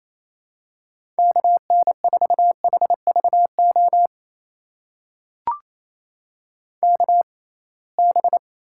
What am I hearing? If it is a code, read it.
Morse code, heard as a keyed tone, about 28 wpm, spaced as for 21 wpm, 700 Hz. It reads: KN4HVO KB
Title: USA Call Signs - 6 Characters - 6x Character Spacing - 28wpm